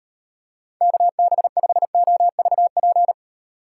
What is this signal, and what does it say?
Morse code, heard as a keyed tone, about 38 wpm, 700 Hz. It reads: KB5OVP